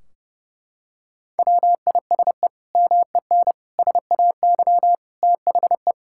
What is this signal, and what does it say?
Morse code, heard as a keyed tone, about 30 wpm, 700 Hz. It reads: WISE MEN SAY THE
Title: Sentences from Top 1600 Words 30wpm